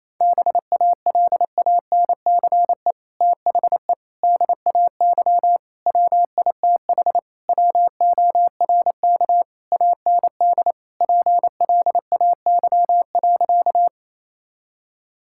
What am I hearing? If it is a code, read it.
Morse code, heard as a keyed tone, about 28 wpm, 700 Hz. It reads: BALANCE THE DAY WITH WORK AND PLAY.
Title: Sentences from Top 1000 Words - Review Part 6 - 28wpm